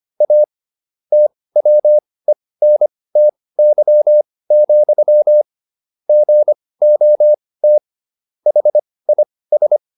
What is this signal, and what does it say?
Morse code, heard as a keyed tone, about 25 wpm, spaced as for 18 wpm, 600 Hz. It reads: A TWENTY, GOT HIS